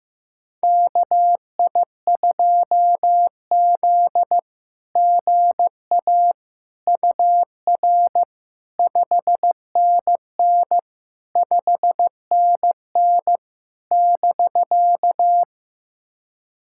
Morse code, heard as a keyed tone, about 15 wpm, 700 Hz. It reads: KI2Z GA UR 5NN 5NN <BK>
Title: SOTA QSO Practice 15wpm